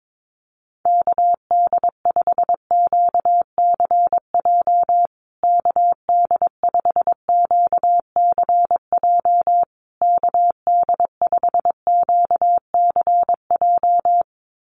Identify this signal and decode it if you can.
Morse code, heard as a keyed tone, about 22 wpm, 700 Hz. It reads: KD5QCJ KD5QCJ KD5QCJ
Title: USA Call Signs Repeated 3x 22wpm